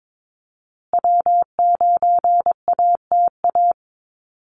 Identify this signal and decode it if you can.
Morse code, heard as a keyed tone, about 22 wpm, 700 Hz. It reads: W9ATA
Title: USA Call Signs Repeated 3x 22wpm